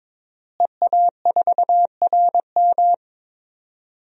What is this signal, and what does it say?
Morse code, heard as a keyed tone, about 22 wpm, 700 Hz. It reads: EA4RM
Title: DXCC-100 Call Signs - 5 Characters - Repeated 3x - 22wpm